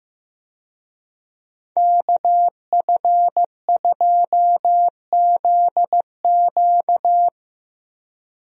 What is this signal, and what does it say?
Morse code, heard as a keyed tone, about 15 wpm, 700 Hz. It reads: KF2ZQ
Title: USA Call Signs - 5 Characters - Repeated 3x - 15wpm